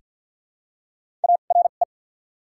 Morse code, heard as a keyed tone, about 54 wpm, 700 Hz. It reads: A R E
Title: 2x Character Spacing - Top 100 Words - 50wpm